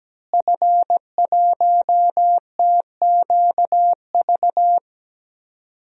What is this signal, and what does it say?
Morse code, heard as a keyed tone, about 17 wpm, 700 Hz. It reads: F1TQV
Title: DXCC-100 Call Signs - 5 Characters - Repeated 2x - 17wpm